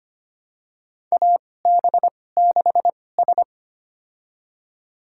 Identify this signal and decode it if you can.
Morse code, heard as a keyed tone, about 25 wpm, spaced as for 18 wpm, 700 Hz. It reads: AB6S
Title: USA Call Signs - 2x Character Spacing - 25wpm